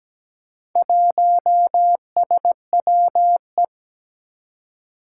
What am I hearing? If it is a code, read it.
Morse code, heard as a keyed tone, about 17 wpm, 700 Hz. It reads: 1SWE
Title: Four Challenging Letter-Number Combinations 17wpm